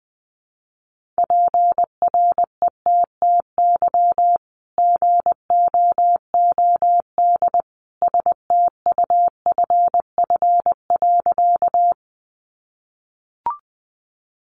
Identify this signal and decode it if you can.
Morse code, heard as a keyed tone, about 20 wpm, 700 Hz. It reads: PRETTY GOOD STUFF.
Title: Sentences from Top 1400 Words 20wpm